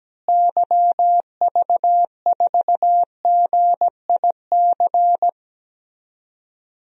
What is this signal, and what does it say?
Morse code, heard as a keyed tone, about 17 wpm, 700 Hz. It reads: YV4GIC
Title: DXCC-100 Call Signs - 6 Characters - Repeated 3x - 17wpm